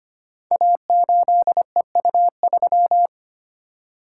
Morse code, heard as a keyed tone, about 25 wpm, 700 Hz. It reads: A8EU3
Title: Code Groups - 5 Letter:Numbers - 25wpm